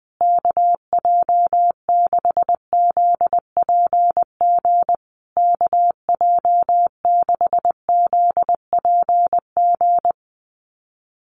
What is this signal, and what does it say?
Morse code, heard as a keyed tone, about 20 wpm, 700 Hz. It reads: KJ6ZPG KJ6ZPG
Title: USA Call Signs Repeated 3x 20wpm